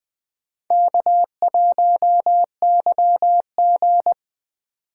Morse code, heard as a keyed tone, about 23 wpm, 700 Hz. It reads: K1YG
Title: Speed Racing - USA 1x2 Call Signs 20wpm